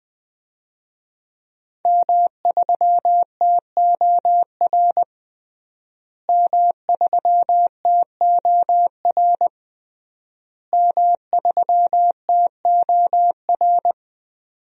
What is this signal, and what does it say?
Morse code, heard as a keyed tone, about 20 wpm, 700 Hz. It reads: M3TOR M3TOR M3TOR
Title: DXCC-100 Call Signs - 5 Characters - Repeated 3x - 20wpm